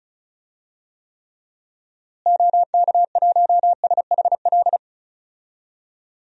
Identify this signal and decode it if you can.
Morse code, heard as a keyed tone, about 34 wpm, 700 Hz. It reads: OK1SHL
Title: DXCC-100 Call Signs - 6 Characters - Repeated 3x - 35wpm